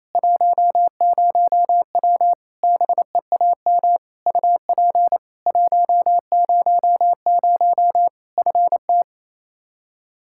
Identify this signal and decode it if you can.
Morse code, heard as a keyed tone, about 28 wpm, 700 Hz. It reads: 10W BEAM UP 100 FT
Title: Practice QSO - Exchange 2 - Info - 28wpm